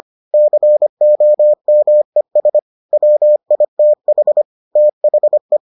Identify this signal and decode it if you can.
Morse code, heard as a keyed tone, about 25 wpm, 600 Hz. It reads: COMES WITH THE